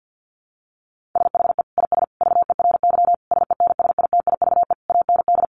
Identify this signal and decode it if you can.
Morse code, heard as a keyed tone, about 50 wpm, 700 Hz. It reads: SHE IS VERY SENSITIVE AND
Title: Sentences from Top 1400 Words 50wpm